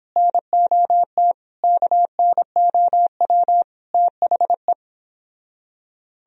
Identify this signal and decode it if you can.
Morse code, heard as a keyed tone, about 29 wpm, 700 Hz. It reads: NOT KNOW THE